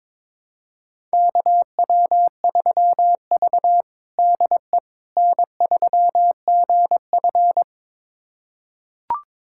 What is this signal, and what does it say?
Morse code, heard as a keyed tone, about 22 wpm, 700 Hz. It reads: KW3V DE N3GF
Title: USA Callsign DE Callsign - 4 Characters - 22wpm